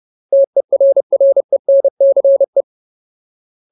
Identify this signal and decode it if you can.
Morse code, heard as a keyed tone, about 30 wpm, 550 Hz. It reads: TERRENCE
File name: Ultimate Word List / 30WPM Ultimate